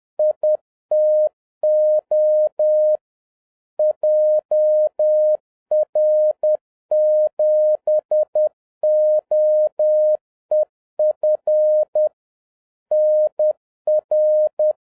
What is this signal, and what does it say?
Morse code, heard as a keyed tone, about 10 wpm, 610 Hz. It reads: ITO JR7OEF NR